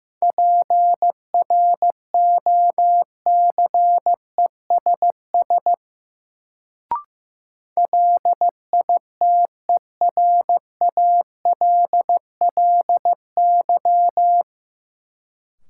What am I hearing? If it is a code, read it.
Morse code, heard as a keyed tone, about 15 wpm, 700 Hz. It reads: PROCESS LITERALLY
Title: Sentences from Top 1400 Words - New Word Drill Alt - 15wpm